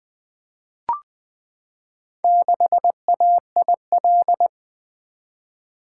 Morse code, heard as a keyed tone, about 20 wpm, 700 Hz. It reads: 6AIL
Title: Four Letter-Number Combinations - 20wpm